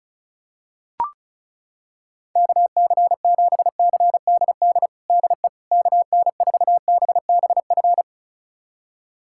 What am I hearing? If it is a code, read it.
Morse code, heard as a keyed tone, about 35 wpm, 700 Hz. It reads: KC7CDD DE KN4BBF
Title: USA Callsign DE Callsign - 6 Characters - 35wpm